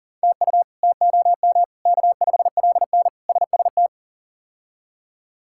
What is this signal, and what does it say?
Morse code, heard as a keyed tone, about 40 wpm, 700 Hz. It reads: TU TOM K5LN SST
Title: K1USN SST QSO Practice - 40wpm